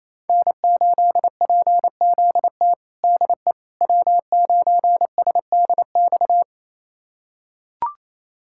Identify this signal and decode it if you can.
Morse code, heard as a keyed tone, about 28 wpm, 700 Hz. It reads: N8PZT DE W9SDX
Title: USA Callsign DE Callsign - 5 Characters - 28wpm